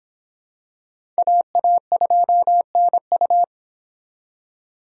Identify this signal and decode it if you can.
Morse code, heard as a keyed tone, about 25 wpm, 700 Hz. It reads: AA2NU